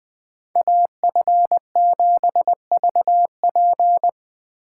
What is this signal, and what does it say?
Morse code, heard as a keyed tone, about 20 wpm, 700 Hz. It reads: AF7VP